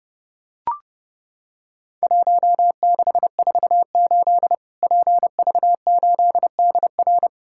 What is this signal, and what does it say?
Morse code, heard as a keyed tone, about 30 wpm, 700 Hz. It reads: 1648 PV8DR
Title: Worked All Europe DX Contest - QTC Exchange - 30wpm